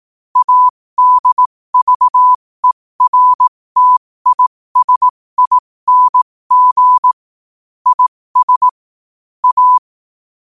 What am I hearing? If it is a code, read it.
Morse code, heard as a keyed tone, about 18 wpm, spaced as for 15 wpm, 1000 Hz. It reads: ADVERTISING IS A